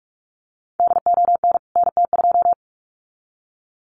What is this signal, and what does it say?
Morse code, heard as a keyed tone, about 44 wpm, 700 Hz. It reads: DON NT2
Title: CWT Exchange 45wpm